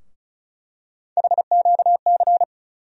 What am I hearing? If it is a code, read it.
Morse code, heard as a keyed tone, about 36 wpm, 700 Hz. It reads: HQC
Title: USA Call Sign Postfixes - 3 Letters 35wpm